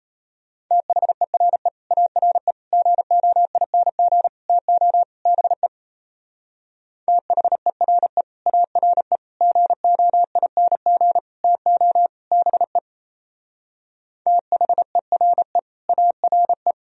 Morse code, heard as a keyed tone, about 34 wpm, 700 Hz. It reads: THERE ARE GOING TO BE THERE ARE GOING TO BE THERE ARE
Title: Speed-Racing - Sets of 5 Words 28wpm